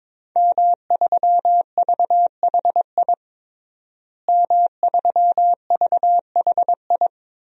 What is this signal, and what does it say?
Morse code, heard as a keyed tone, about 22 wpm, 700 Hz. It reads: M3VHI M3VHI
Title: DXCC-100 Call Signs - 5 Characters - Repeated 2x - 22wpm